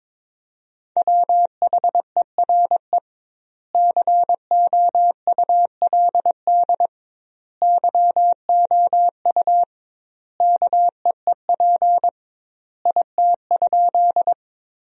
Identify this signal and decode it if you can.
Morse code, heard as a keyed tone, about 22 wpm, 700 Hz. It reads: WHERE COULD YOU KEEP IT?
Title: 2x Word-Spacing - Sentences from Top 300 Words - 22wpm